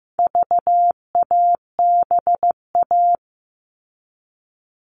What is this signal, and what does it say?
Morse code, heard as a keyed tone, about 15 wpm, 700 Hz. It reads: VABA
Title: All Cut Numbers - 4-Digit 15wpm